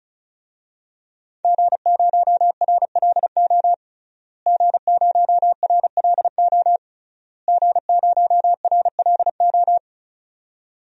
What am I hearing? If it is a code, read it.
Morse code, heard as a keyed tone, about 36 wpm, 700 Hz. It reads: G0RLO G0RLO G0RLO
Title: DXCC-100 Call Signs - 5 Characters - Repeated 3x - 35wpm